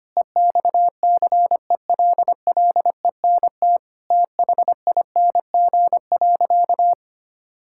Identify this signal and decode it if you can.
Morse code, heard as a keyed tone, about 25 wpm, 700 Hz. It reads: EXCELLENT THING.
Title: Sentences from Top 1000 Words - Review Part 5 - 25wpm